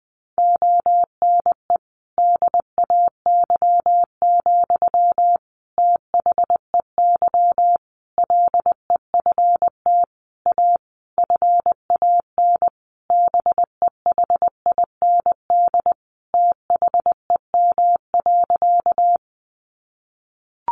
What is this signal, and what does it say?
Morse code, heard as a keyed tone, about 20 wpm, 700 Hz. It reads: ONE DAY, THEY LEFT A FAN BEHIND THEM.
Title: Sentences from Top 1300 Words 20wpm